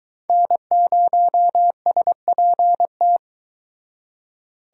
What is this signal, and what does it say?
Morse code, heard as a keyed tone, about 23 wpm, 700 Hz. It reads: N0SPT